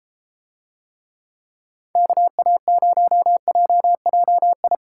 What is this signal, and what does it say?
Morse code, heard as a keyed tone, about 32 wpm, 700 Hz. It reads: KA0JJI